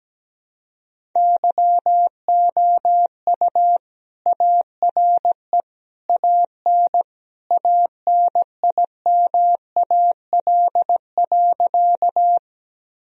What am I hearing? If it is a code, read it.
Morse code, heard as a keyed tone, about 17 wpm, 700 Hz. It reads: YOU ARE AN ANIMAL.
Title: Sentences from Top 1000 Words - Review Part 1 - 17wpm